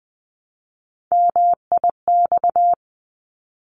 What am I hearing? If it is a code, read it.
Morse code, heard as a keyed tone, about 20 wpm, 700 Hz. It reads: MIX
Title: 260 Lesson 42e - Words 20-20wpm